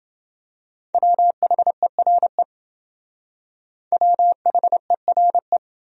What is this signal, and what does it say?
Morse code, heard as a keyed tone, about 29 wpm, 700 Hz. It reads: WHERE WHERE